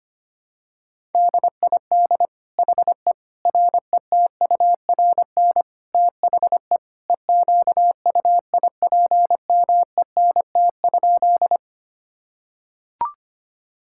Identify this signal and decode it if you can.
Morse code, heard as a keyed tone, about 25 wpm, 700 Hz. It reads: DID HE RETURN THE EQUIPMENT?
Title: Sentences from Top 1000 Words - Review Part 5 - 25wpm